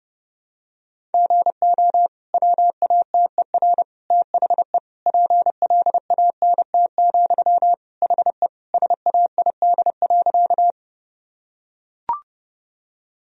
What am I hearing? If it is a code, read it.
Morse code, heard as a keyed tone, about 30 wpm, 700 Hz. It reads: GO WATER THE PLANT, HE SAID.
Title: Sentences from Top 1000 Words - Review Part 1 - 30wpm